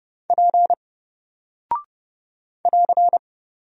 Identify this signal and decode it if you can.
Morse code, heard as a keyed tone, about 30 wpm, 700 Hz. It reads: P <AR>
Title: Single Character (Letter:Number:Punctuation:Prosign) 30wpm